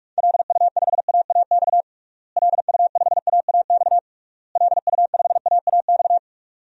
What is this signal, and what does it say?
Morse code, heard as a keyed tone, about 41 wpm, 700 Hz. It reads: LU5AAX LU5AAX LU5AAX
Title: DXCC-100 Call Signs - 6 Characters - Repeated 3x - 45wpm